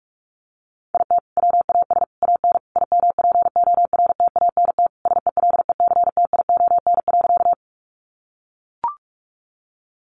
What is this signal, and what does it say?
Morse code, heard as a keyed tone, about 44 wpm, 700 Hz. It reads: IT WAS AN IMPORTANT SELECTION.
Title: Sentences from Top 1400 Words 45wpm